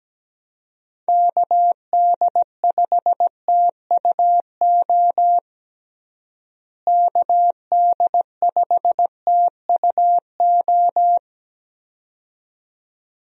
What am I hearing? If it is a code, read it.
Morse code, heard as a keyed tone, about 17 wpm, 700 Hz. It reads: KD5TUO KD5TUO
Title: USA Call Signs - 6 Characters - Repeated 2x - 17wpm